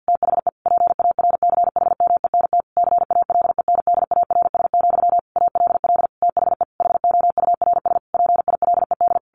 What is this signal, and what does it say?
Morse code, heard as a keyed tone, about 50 wpm, 700 Hz. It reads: THE PARCHMENT CALENDARS, ALL THE HOURS PILED
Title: A Sound Of Thunder 50wpm